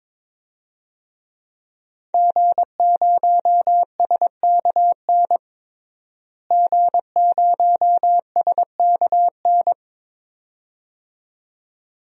Morse code, heard as a keyed tone, about 22 wpm, 700 Hz. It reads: G0SKN G0SKN
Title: DXCC-100 Call Signs - 5 Characters - Repeated 2x - 22wpm